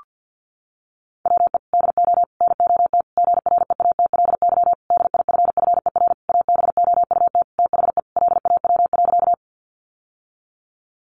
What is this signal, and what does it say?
Morse code, heard as a keyed tone, about 50 wpm, 700 Hz. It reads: WE DO NOT GREATLY DIFFER ABOUT THE LAW.
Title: Sentences from Top 1400 Words 50wpm